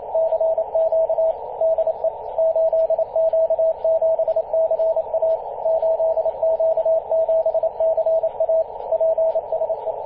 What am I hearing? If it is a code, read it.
Morse code, heard as a keyed tone, about 29 wpm, 650 Hz, with radio noise. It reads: CQ DE 8Q7CA 8Q7CA PSE
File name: 8q7ca-21-013mhz-cw